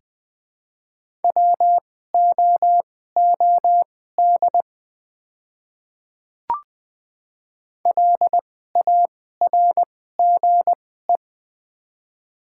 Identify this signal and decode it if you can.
Morse code, heard as a keyed tone, about 20 wpm, spaced as for 15 wpm, 700 Hz. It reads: WOOD LARGE
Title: 2x Character Spacing - Top 300 Words - 20wpm